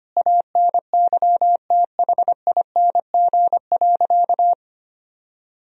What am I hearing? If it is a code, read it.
Morse code, heard as a keyed tone, about 25 wpm, 700 Hz. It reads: ANYTHING.
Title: Sentences from Top 1000 Words - Review Part 2 - 25wpm